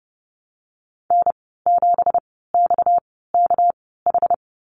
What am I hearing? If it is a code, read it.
Morse code, heard as a keyed tone, about 30 wpm, spaced as for 17 wpm, 700 Hz. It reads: N7XKH
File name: Lesson 37g - Letter x - Call Signs 30wpm